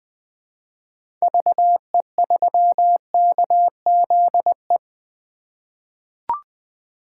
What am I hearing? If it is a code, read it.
Morse code, heard as a keyed tone, about 20 wpm, 700 Hz. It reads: VE3KZE
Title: 2023 POTA Activator Call Signs - 20wpm